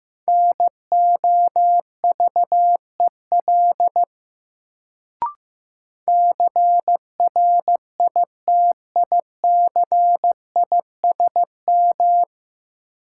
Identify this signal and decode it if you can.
Morse code, heard as a keyed tone, about 15 wpm, 700 Hz. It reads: NOVEL CRITICISM
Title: Sentences from Top 1300 Words - New Word Drill Alt - 15wpm